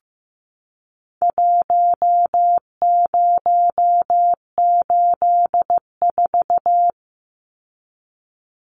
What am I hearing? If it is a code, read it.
Morse code, heard as a keyed tone, about 15 wpm, 700 Hz. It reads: 1084
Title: Numbers - Contest Style - 4-Digit 15wpm